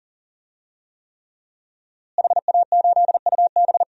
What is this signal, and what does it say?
Morse code, heard as a keyed tone, about 38 wpm, 700 Hz. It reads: HA8UB
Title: DXCC-100 Call Signs - 5 Characters - Repeated 3x - 40wpm